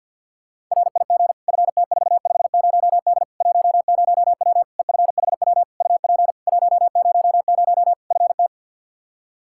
Morse code, heard as a keyed tone, about 48 wpm, 700 Hz. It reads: RIG FT450D 10W EFHW UP 100 FT
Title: K1USN SST QSO Practice 50wpm